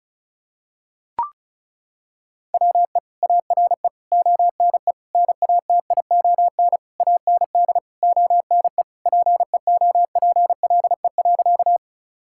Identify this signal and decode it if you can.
Morse code, heard as a keyed tone, about 34 wpm, 700 Hz. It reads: WE ARE ONE NATION AND ONE PEOPLE.
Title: Sentences from Top 1000 Words - Review Part 6 - 35wpm